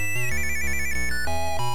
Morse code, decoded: none